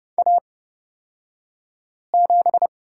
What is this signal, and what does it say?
Morse code, heard as a keyed tone, about 31 wpm, spaced as for 5 wpm, 700 Hz. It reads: A7
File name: Lesson 36d - BT - Two Characters 30wpm